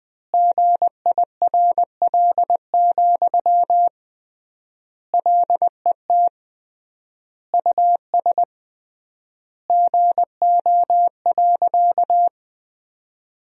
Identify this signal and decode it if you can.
Morse code, heard as a keyed tone, about 21 wpm, 700 Hz. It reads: GIRL, LET US GO.
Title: 3x Word-Spacing - Sentences from Top 400 Words - 20wpm